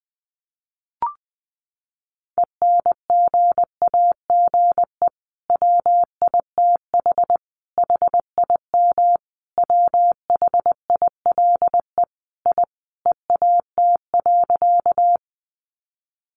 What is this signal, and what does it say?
Morse code, heard as a keyed tone, about 20 wpm, 700 Hz. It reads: ENGAGE WITH HIM WHILE I EAT.
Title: Sentences from Top 1500 Words 20wpm